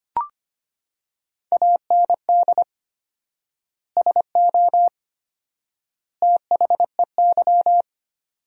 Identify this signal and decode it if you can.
Morse code, heard as a keyed tone, about 25 wpm, 700 Hz. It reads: AND SO THEY
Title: 4x Word-Spacing - Sentences from Top 400 Words - 25wpm